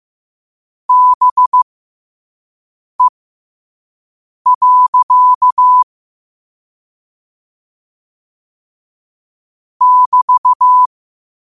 Morse code, heard as a keyed tone, about 15 wpm, spaced as for 5 wpm, 1000 Hz. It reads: BE. <BT>
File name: qotdfiles / 5wpm